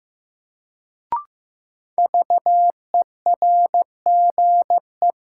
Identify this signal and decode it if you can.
Morse code, heard as a keyed tone, about 15 wpm, 700 Hz. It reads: VERGE
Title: Sets of 4 Words - New Word Drill - 15wpm